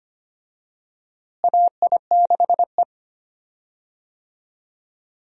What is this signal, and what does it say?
Morse code, heard as a keyed tone, about 25 wpm, 700 Hz. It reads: AI6E
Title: USA Call Signs - 4 Characters - Repeated 3x - 25wpm